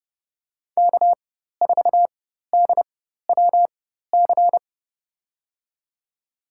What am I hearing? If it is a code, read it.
Morse code, heard as a keyed tone, about 30 wpm, spaced as for 14 wpm, 700 Hz. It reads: K4DWC
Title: USA Call Signs - 4x Character Spacing - 30wpm